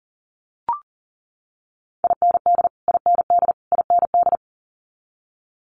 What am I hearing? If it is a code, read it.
Morse code, heard as a keyed tone, about 37 wpm, 700 Hz. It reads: IND IND IND
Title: Multi-letter Phonemes Repeated 3x 40wpm